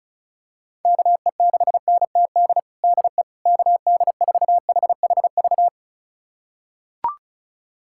Morse code, heard as a keyed tone, about 36 wpm, 700 Hz. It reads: KE6NTD DE KD4HHV
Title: USA Callsign DE Callsign 35wpm